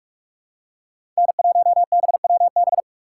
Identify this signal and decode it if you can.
Morse code, heard as a keyed tone, about 46 wpm, 700 Hz. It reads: N1BWB
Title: USA Call Signs - 4x Character Spacing - 45wpm